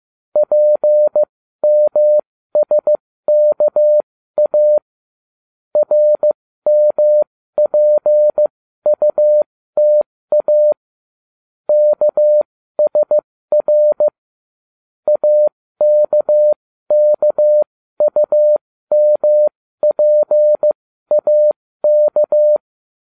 Morse code, heard as a keyed tone, about 15 wpm, spaced as for 13 wpm, 600 Hz. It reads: PMSKA RMPUTA KSR AKKUMPAK